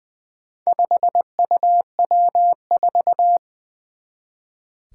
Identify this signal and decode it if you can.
Morse code, heard as a keyed tone, about 20 wpm, 700 Hz. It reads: 5UW4
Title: Four Challenging Letter-Number Combinations 20wpm